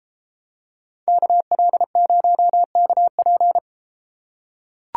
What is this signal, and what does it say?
Morse code, heard as a keyed tone, about 32 wpm, 700 Hz. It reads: KL0KP